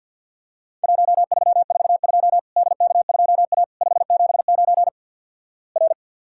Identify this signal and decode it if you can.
Morse code, heard as a keyed tone, about 48 wpm, 700 Hz. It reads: 1342 DK2A 579 R
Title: Worked All Europe DX Contest - QTC Exchange - 50wpm